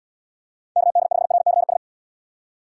Morse code, heard as a keyed tone, about 76 wpm, 700 Hz. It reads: KD5RCS